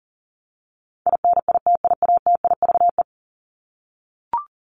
Morse code, heard as a keyed tone, about 37 wpm, 700 Hz. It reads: INITIATIVE
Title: Top 1751-2000 Words 40wpm